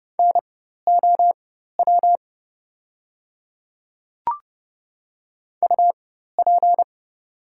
Morse code, heard as a keyed tone, about 30 wpm, spaced as for 14 wpm, 700 Hz. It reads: NOW UP
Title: 4x Character Spacing - Top 100 Words - 30wpm